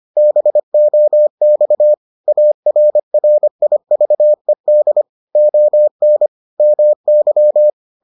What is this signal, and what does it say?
Morse code, heard as a keyed tone, about 25 wpm, 600 Hz. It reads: BOX ARRIVED ON MY